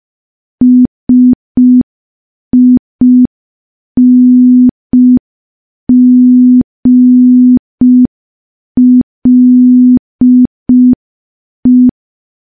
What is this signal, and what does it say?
Morse code, heard as a keyed tone, about 5 wpm, 250 Hz. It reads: SINGLE
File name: morse / morse-training-05WPM